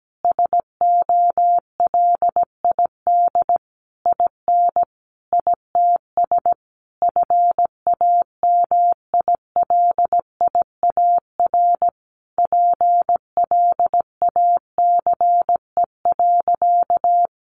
Morse code, heard as a keyed tone, about 17 wpm, 700 Hz. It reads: SOLID IN ITS FAMILIAR PLACE.
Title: Sentences from Top 1500 Words 17wpm